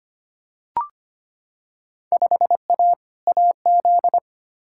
Morse code, heard as a keyed tone, about 25 wpm, 700 Hz. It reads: 5A AZ
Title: Field Day Exchanges - 25wpm